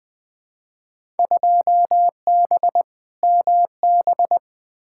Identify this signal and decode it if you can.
Morse code, heard as a keyed tone, about 20 wpm, 700 Hz. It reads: 2B MB
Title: Field Day Exchanges Encore - 20wpm